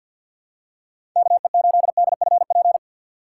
Morse code, heard as a keyed tone, about 47 wpm, 700 Hz. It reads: KE8DFP
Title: USA Call Signs - 6 Characters - 6x Character Spacing - 50wpm